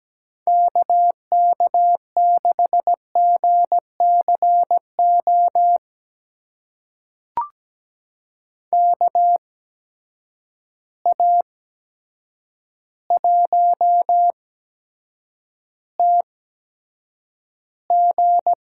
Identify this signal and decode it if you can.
Morse code, heard as a keyed tone, about 17 wpm, spaced as for 8 wpm, 700 Hz. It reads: KK6GCO KA1TG
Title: USA Call Signs - 8x Character Spacing - 17wpm